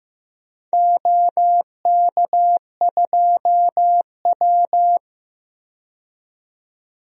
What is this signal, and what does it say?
Morse code, heard as a keyed tone, about 15 wpm, 700 Hz. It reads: OK2W
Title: DXCC-100 Call Signs - 4 Characters - 15wpm